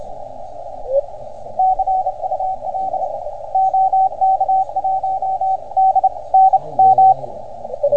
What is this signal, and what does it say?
Morse code, heard as a keyed tone, about 33 wpm, spaced as for 27 wpm, 760 Hz, with radio noise. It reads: CUL OK1DNM